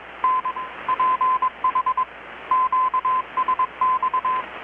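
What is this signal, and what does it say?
Morse code, heard as a keyed tone, about 21 wpm, 1020 Hz, with radio noise. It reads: N PH QSX
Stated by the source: Morse from KPH's Night of Nights XX maritime radiotelegraphy event, received in Chiapas, Mexico about 3,800 km away on a Kenwood R-600 with a grounded 10 m long-wire antenna.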